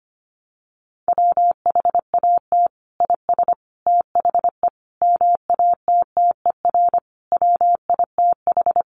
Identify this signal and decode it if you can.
Morse code, heard as a keyed tone, about 25 wpm, 700 Hz. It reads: WHAT IS THE MATTER WITH
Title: Sentences from Top 700 Words 25wpm